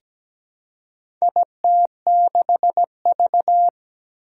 Morse code, heard as a keyed tone, about 17 wpm, 700 Hz. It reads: IT6V
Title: Four Challenging Letter-Number Combinations 17wpm